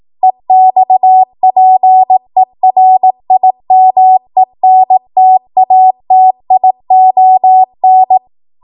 Morse code, heard as a keyed tone, about 18 wpm, 750 Hz. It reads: EXPERIMENTATION
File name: Morse / 18WPM